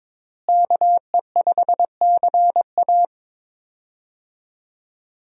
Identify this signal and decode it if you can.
Morse code, heard as a keyed tone, about 22 wpm, 700 Hz. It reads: KE5CA
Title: USA Call Signs - 5 Characters - Repeated 2x - 22wpm